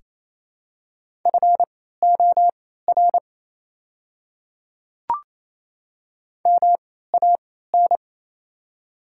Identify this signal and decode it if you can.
Morse code, heard as a keyed tone, about 28 wpm, spaced as for 16 wpm, 700 Hz. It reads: FOR MAN
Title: 3x Character Spacing - Top 300 Words - 28wpm